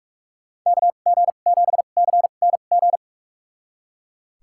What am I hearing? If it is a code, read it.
Morse code, heard as a keyed tone, about 41 wpm, spaced as for 31 wpm, 700 Hz. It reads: KG7CNG
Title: USA Call Signs - 2x Character Spacing - 45wpm